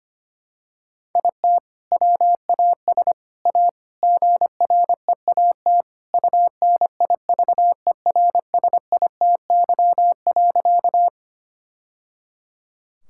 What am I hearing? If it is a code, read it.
Morse code, heard as a keyed tone, about 25 wpm, 700 Hz. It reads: IT WAS A GREAT UNIVERSITY.
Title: Sentences from Top 1000 Words - Review Part 5 - 25wpm